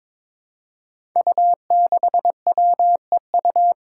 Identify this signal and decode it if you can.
Morse code, heard as a keyed tone, about 21 wpm, 700 Hz. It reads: U6WEU
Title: Five Challenging Letter-Number Combinations 22wpm